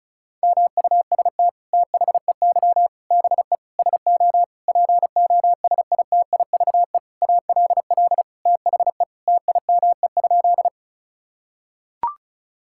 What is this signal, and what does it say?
Morse code, heard as a keyed tone, about 36 wpm, 700 Hz. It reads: MUST THEY BE SO POSITIVE ALL THE TIME?
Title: Sentences from Top 1000 Words - Review Part 5 - 35wpm